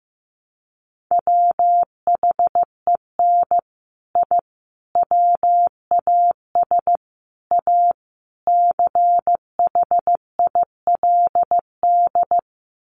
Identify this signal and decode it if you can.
Morse code, heard as a keyed tone, about 14 wpm, 700 Hz. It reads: WHEN I WAS A CHILD
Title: Sets of 5 Words Encore 15wpm